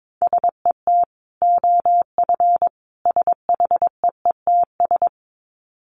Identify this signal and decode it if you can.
Morse code, heard as a keyed tone, about 22 wpm, 700 Hz. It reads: SET OF SHEETS
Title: Dit Challenge 22wpm